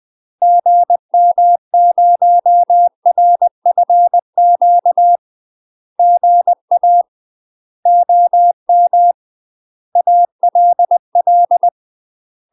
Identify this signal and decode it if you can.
Morse code, heard as a keyed tone, about 20 wpm, 700 Hz. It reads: GM0RFQ GA OM ALL
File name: UK MASTER QSOs TRAINING / 20WPM / UK MASTER QSOs 0023